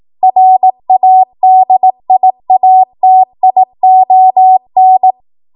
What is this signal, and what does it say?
Morse code, heard as a keyed tone, about 18 wpm, 750 Hz. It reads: RADIATION
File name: Morse / 18WPM